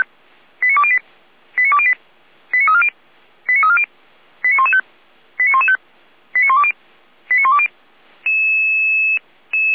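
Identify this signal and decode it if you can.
Five-tone selective call (ZVEI), heard as quick runs of five tones, tones of about 70 ms. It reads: 89208 89208 89340 89340 89115 89115 89120 89120
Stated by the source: ZVEI five-tone selective call, code 89094, received by SDR.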